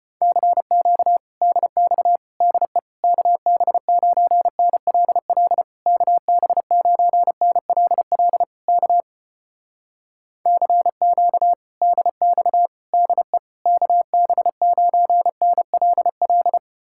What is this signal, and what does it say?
Morse code, heard as a keyed tone, about 34 wpm, 700 Hz. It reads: CQ DX DE KB9NLL KB9NLL K CQ DX DE KB9NLL